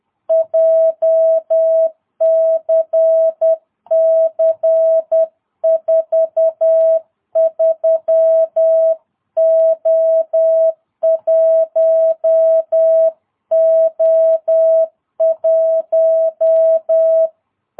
Morse code, heard as a keyed tone, about 10 wpm, 660 Hz. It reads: JCC 43O1O1
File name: jh8cbh / 2022cw